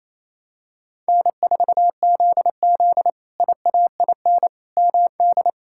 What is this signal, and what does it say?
Morse code, heard as a keyed tone, about 28 wpm, 700 Hz. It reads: N4ZZ IAIN MD
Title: K1USN SST QSO Practice - 28wpm